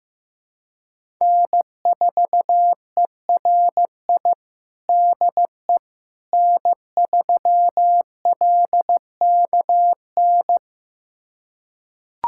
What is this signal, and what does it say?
Morse code, heard as a keyed tone, about 15 wpm, 700 Hz. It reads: N4ERI DE N3LKN
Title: USA Callsign DE Callsign - 5 Characters - 15wpm